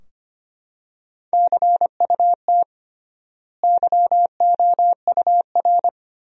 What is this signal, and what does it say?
Morse code, heard as a keyed tone, about 25 wpm, 700 Hz. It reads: CUT YOUR
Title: 3x Word-Spacing - Sentences from Top 300 Words - 25wpm